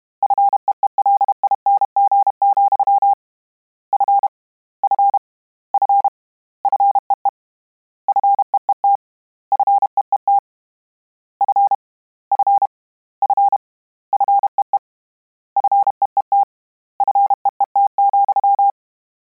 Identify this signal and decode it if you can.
Morse code, heard as a keyed tone, about 32 wpm, 800 Hz. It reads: FEELING, F F F FEE FEET FEET F F F FEE FEET FEET,
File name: Combinations / Combinations-mp3s / Combinations@30